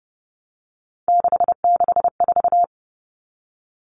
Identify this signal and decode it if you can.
Morse code, heard as a keyed tone, about 30 wpm, 700 Hz. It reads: 664
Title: Common Cut Numbers - 3-Digit 30wpm